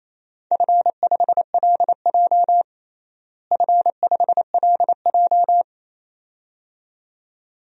Morse code, heard as a keyed tone, about 28 wpm, 700 Hz. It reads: F5LJ F5LJ
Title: DXCC-100 Call Signs - 4 Characters - Repeated 2x - 28wpm